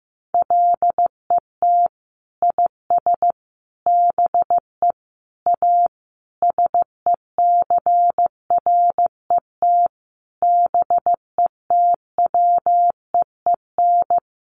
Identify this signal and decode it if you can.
Morse code, heard as a keyed tone, about 15 wpm, 700 Hz. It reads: LET IS BE A SECRET BETWEEN
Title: Sentences from Top 1200 Words 15wpm